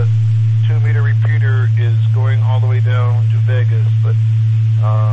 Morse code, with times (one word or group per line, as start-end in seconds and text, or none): none